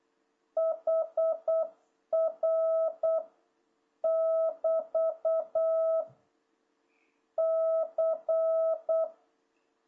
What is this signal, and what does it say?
Morse code, heard as a keyed tone, about 8 wpm, 660 Hz. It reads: HR <BT> C
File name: jh8cbh / 2026cw